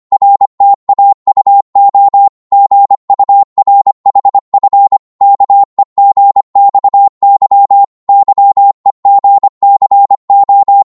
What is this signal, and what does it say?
Morse code, heard as a keyed tone, about 25 wpm, 800 Hz. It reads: RTAUO GURHF KEGXY YEGCO
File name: Morse / ltr-25wpm